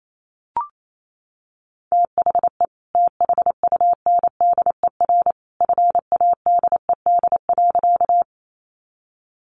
Morse code, heard as a keyed tone, about 28 wpm, 700 Hz. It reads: THE THUNDER FADED.
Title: A Sound Of Thunder 28wpm